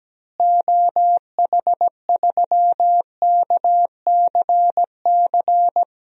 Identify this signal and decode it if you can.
Morse code, heard as a keyed tone, about 17 wpm, 700 Hz. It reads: OH3KCC
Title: DXCC-100 Call Signs - 6 Characters - Repeated 3x - 17wpm